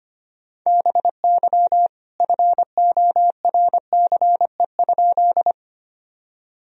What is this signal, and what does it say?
Morse code, heard as a keyed tone, about 25 wpm, 700 Hz. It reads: BY FORCE?
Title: Sentences from Top 1000 Words - Review Part 3 - 25wpm